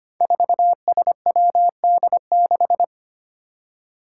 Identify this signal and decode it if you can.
Morse code, heard as a keyed tone, about 25 wpm, 700 Hz. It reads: 4SWD6
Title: Five Challenging Letter-Number Combinations 25wpm